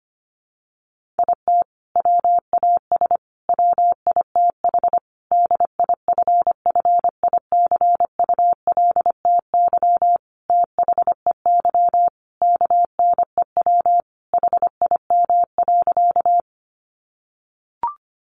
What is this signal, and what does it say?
Morse code, heard as a keyed tone, about 25 wpm, 700 Hz. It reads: IT WAS WITH DIFFICULTY THEY KNEW HIM.
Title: Sentences from Top 1500 Words 25wpm